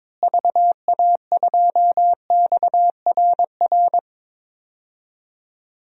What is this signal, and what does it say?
Morse code, heard as a keyed tone, about 22 wpm, 700 Hz. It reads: VA2XRR
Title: DXCC-100 Call Signs - 6 Characters - Repeated 3x - 22wpm